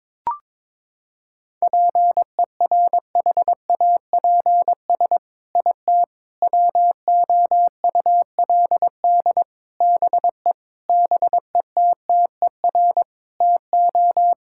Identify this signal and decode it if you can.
Morse code, heard as a keyed tone, about 22 wpm, 700 Hz. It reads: PERHAPS IT WOULD BE BETTER TO
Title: Sentences from Top 1600 Words 22wpm